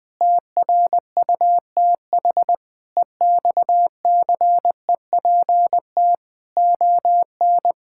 Morse code, heard as a keyed tone, about 21 wpm, 700 Hz. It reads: TRUTH EXCEPT ON
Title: Sentences from Top 1600 Words 20wpm